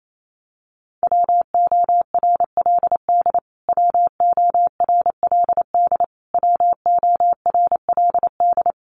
Morse code, heard as a keyed tone, about 28 wpm, 700 Hz. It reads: WORLD WORLD WORLD
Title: Top 300 Words Repeated 28wpm